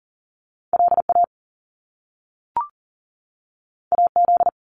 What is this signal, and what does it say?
Morse code, heard as a keyed tone, about 39 wpm, 700 Hz. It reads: LA AZ
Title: US States 40wpm